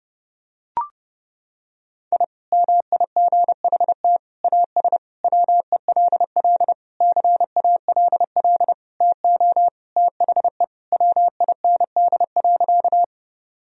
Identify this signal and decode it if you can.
Morse code, heard as a keyed tone, about 32 wpm, 700 Hz. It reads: I MIGHT AS WELL CALL TO THE WIND.
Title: Sentences from Top 1000 Words - Review Part 2 - 30wpm